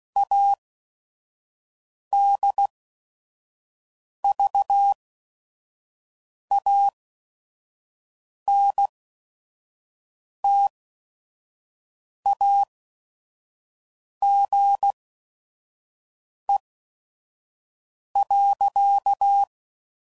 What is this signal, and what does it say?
Morse code, heard as a keyed tone, about 16 wpm, spaced as for 5 wpm, 800 Hz. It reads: ADVANTAGE.
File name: CW Code Groups @5WPM